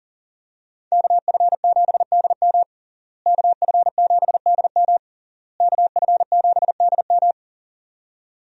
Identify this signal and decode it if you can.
Morse code, heard as a keyed tone, about 38 wpm, 700 Hz. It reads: KF7DM KF7DM KF7DM
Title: USA Call Signs - 5 Characters - Repeated 3x - 40wpm